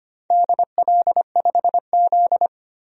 Morse code, heard as a keyed tone, about 25 wpm, 700 Hz. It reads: DL5Z
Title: DXCC-100 Call Signs - 4 Characters - 25wpm